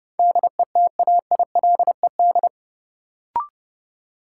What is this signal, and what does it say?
Morse code, heard as a keyed tone, about 30 wpm, 700 Hz. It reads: DETAILED
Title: Sentences from Top 1500 Words - New Word Drill Alt - 30wpm